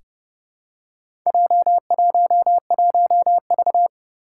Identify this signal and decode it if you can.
Morse code, heard as a keyed tone, about 28 wpm, 690 Hz. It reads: J11V
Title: Four Challenging Letter-Number Combinations 30wpm